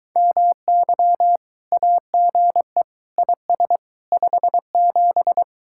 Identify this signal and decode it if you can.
Morse code, heard as a keyed tone, about 23 wpm, 700 Hz. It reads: MY AGE IS 57